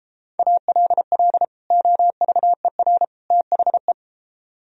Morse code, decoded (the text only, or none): ALL OVER THE